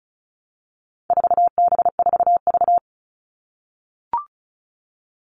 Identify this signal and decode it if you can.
Morse code, heard as a keyed tone, about 34 wpm, 700 Hz. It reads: 4B4V
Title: All Cut Numbers - 4-Digit 35wpm